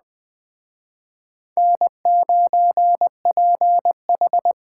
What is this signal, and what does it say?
Morse code, heard as a keyed tone, about 24 wpm, 700 Hz. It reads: N9PH
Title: Speed Racing - USA 1x2 Call Signs 20wpm